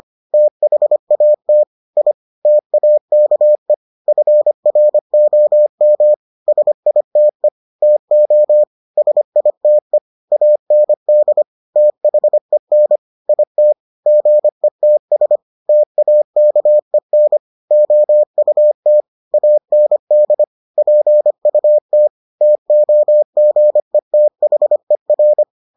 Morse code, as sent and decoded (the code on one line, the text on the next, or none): - .... .- - / .. / - .- -.- . / ..-. .-. --- -- / ... .. - . / - --- / ... .. - . / .- -. -.. / - .... -. / .. - / --. - ... / - .- -.- . -. / --- ..- - / .- -. -.. / .--. ..- - / - --- --. - .... .-.
THAT I TAKE FROM SITE TO SITE AND THEN IT GETS TAKEN OUT AND PUT TOGETHER